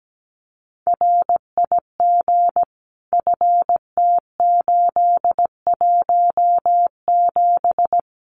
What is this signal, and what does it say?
Morse code, heard as a keyed tone, about 17 wpm, 700 Hz. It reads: RIG FT817
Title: Common QSO Phrases Encore 17wpm